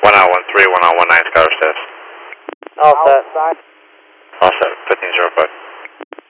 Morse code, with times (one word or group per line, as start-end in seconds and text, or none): none